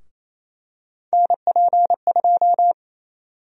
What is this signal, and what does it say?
Morse code, heard as a keyed tone, about 28 wpm, 700 Hz. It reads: NP2
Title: USA Call Sign Prefixes - 28wpm